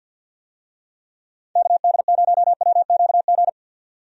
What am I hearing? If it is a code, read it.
Morse code, heard as a keyed tone, about 51 wpm, 700 Hz. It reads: KD0WQG
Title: USA Call Signs - 6 Characters - 50wpm